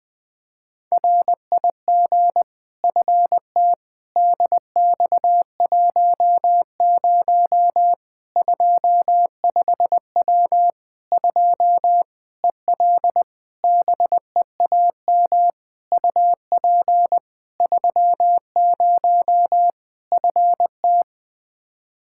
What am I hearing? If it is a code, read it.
Morse code, heard as a keyed tone, about 20 wpm, 700 Hz. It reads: RIG FT DX10 25W 2 EL BEAM UP 30 FT
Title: Practice QSO - Exchange 2 - Info - 20wpm